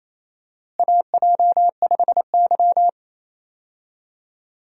Morse code, heard as a keyed tone, about 28 wpm, 700 Hz. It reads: AJ5Y
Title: USA Call Signs - 4 Characters - Repeated 3x - 28wpm